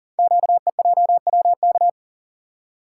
Morse code, heard as a keyed tone, about 40 wpm, 700 Hz. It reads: QEJWK
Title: Code Groups - 5 Letters - 40wpm